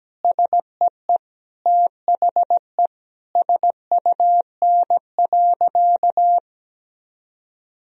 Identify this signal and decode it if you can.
Morse code, heard as a keyed tone, about 17 wpm, 700 Hz. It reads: SEE THE SUN.
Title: Sentences from Top 1000 Words - Review Part 1 - 17wpm